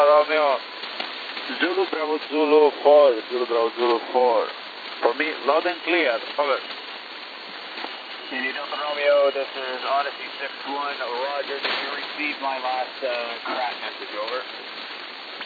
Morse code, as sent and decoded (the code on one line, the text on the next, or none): none